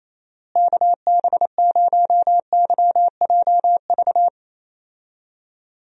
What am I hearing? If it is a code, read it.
Morse code, heard as a keyed tone, about 27 wpm, 700 Hz. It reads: KB0YJV
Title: USA Call Signs - 6 Characters - Repeated 2x - 28wpm